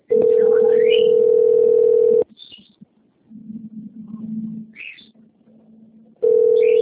Ringback tone (North America), 440 + 480 Hz.